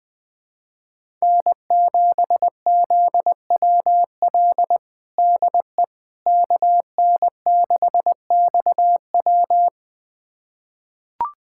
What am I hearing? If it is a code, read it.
Morse code, heard as a keyed tone, about 20 wpm, 700 Hz. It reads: N7ZWL DE KN6XW
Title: USA Callsign DE Callsign - 5 Characters - 20wpm